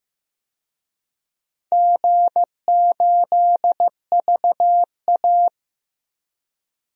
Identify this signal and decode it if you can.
Morse code, heard as a keyed tone, about 14 wpm, 700 Hz. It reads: G8VA
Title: DXCC-100 Call Signs - 4 Characters - Repeated 2x - 15wpm